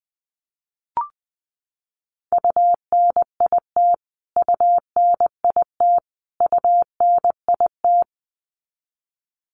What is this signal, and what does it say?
Morse code, heard as a keyed tone, about 19 wpm, 700 Hz. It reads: UNIT UNIT UNIT
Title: Top 500 Words Repeated 20wpm